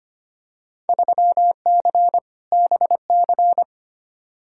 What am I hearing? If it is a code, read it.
Morse code, heard as a keyed tone, about 25 wpm, 700 Hz. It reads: 3C BC
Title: Field Day Exchanges Encore - 25wpm